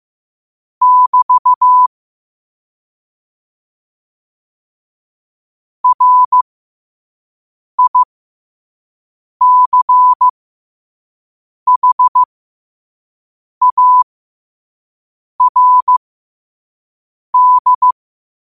Morse code, heard as a keyed tone, about 15 wpm, spaced as for 5 wpm, 1000 Hz. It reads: <BT> RICHARD
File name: qotdfiles / 5wpm